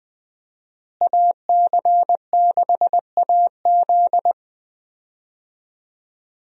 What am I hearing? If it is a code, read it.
Morse code, heard as a keyed tone, about 20 wpm, 700 Hz. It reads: AC6AZ
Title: USA Call Signs - 5 Characters - Repeated 3x - 20wpm